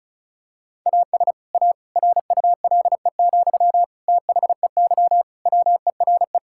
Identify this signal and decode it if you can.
Morse code, heard as a keyed tone, about 34 wpm, 700 Hz. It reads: AS A RULE, THEY WERE
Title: Sentences from Top 1000 Words - Review Part 6 - 35wpm